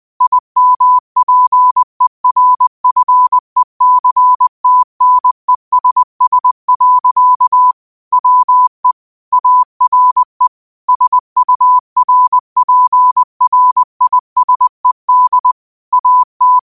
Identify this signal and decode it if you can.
Morse code, heard as a keyed tone, about 20 wpm, 1000 Hz. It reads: IMPERFECTNESS. WE ARE SURPRISED AT
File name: qotdfiles / 20wpm